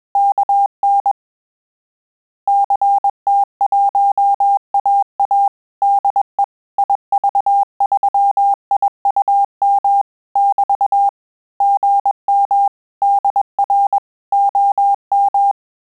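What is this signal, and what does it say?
Morse code, heard as a keyed tone, about 21 wpm, 800 Hz. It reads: KN CT1AA DE IV3IUM <BT> GM DR OM
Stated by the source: Morse QSO practice at 20 words per minute, in Italian and English.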